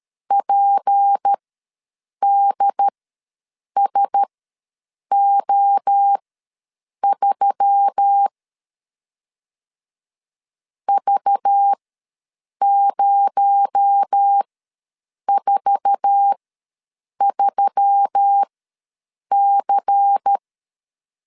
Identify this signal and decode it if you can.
Morse code, heard as a keyed tone, about 13 wpm, spaced as for 7 wpm, 800 Hz. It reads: PDSO3 V043C
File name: LEZIONI / 8 YJ45 / 8 YJ4 ws 12wpm 3249